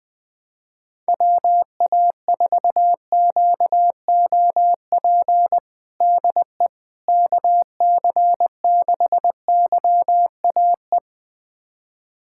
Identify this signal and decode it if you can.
Morse code, heard as a keyed tone, about 20 wpm, 700 Hz. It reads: WA4QOP DE KC6YAE
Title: USA Callsign DE Callsign 20wpm